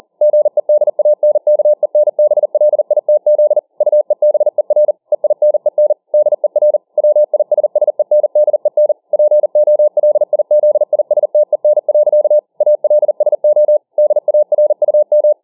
Morse code, heard as a keyed tone, about 40 wpm, 600 Hz. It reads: GEDANKENBLITZ UEBER EINEN DER WISSENDEN POLIZISTEN. ALSO DARUM